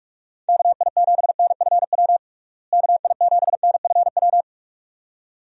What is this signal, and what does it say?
Morse code, heard as a keyed tone, about 46 wpm, 700 Hz. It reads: KI7NFW KI7NFW
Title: USA Call Signs - 6 Characters - Repeated 2x - 45wpm